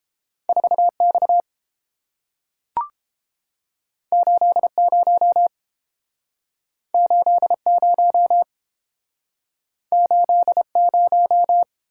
Morse code, heard as a keyed tone, about 31 wpm, 700 Hz. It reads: 4X 80 80 80